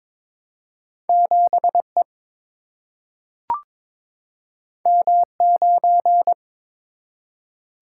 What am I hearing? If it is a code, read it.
Morse code, heard as a keyed tone, about 22 wpm, 700 Hz. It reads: 7 E M 9
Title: Single-Letter Single-Number 22wpm